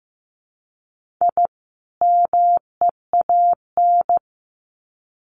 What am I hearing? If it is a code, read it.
Morse code, heard as a keyed tone, about 15 wpm, 700 Hz. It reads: I MEAN
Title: Sets of 2 Words 15wpm